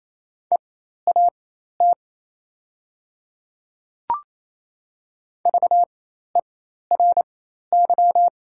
Morse code, heard as a keyed tone, about 28 wpm, spaced as for 13 wpm, 700 Hz. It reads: EAT VERY
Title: 4x Character Spacing - Top 300 Words - 28wpm